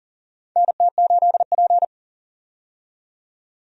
Morse code, heard as a keyed tone, about 38 wpm, 700 Hz. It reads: NT8P
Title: USA Call Signs - 4 Characters - 40wpm